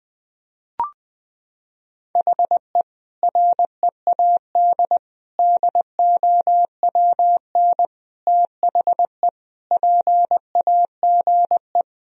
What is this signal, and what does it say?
Morse code, heard as a keyed tone, about 21 wpm, 700 Hz. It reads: HE READ DOWN THE PAGE
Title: Sentences from Top 1000 Words - Review Part 1 - 20wpm